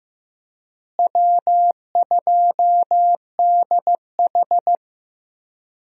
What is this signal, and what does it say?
Morse code, heard as a keyed tone, about 15 wpm, 700 Hz. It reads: W2DH
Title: USA Call Signs - 4 Characters - Repeated 3x - 15wpm